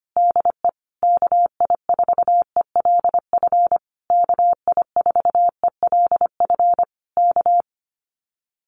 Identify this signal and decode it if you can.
Morse code, heard as a keyed tone, about 25 wpm, 700 Hz. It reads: DE KI4ELF KI4ELF K
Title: Common QSO Phrases Encore 25wpm